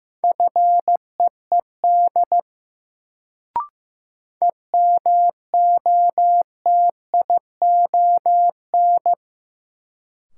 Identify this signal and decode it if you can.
Morse code, heard as a keyed tone, about 15 wpm, 700 Hz. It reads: FEED EMOTION
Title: Sentences from Top 1600 Words - New Word Drill - 15wpm